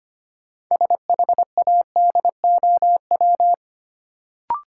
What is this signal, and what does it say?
Morse code, heard as a keyed tone, about 25 wpm, 700 Hz. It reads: SHADOW
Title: Sets of 5 Words Encore - New Word Drill - 25wpm